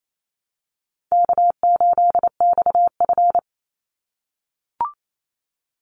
Morse code, heard as a keyed tone, about 28 wpm, 700 Hz. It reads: K8XF
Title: Call Signs - 28wpm